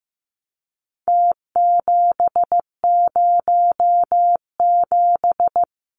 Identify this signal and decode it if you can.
Morse code, heard as a keyed tone, about 15 wpm, 700 Hz. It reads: T707
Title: Numbers - Contest Style - 4-Digit 15wpm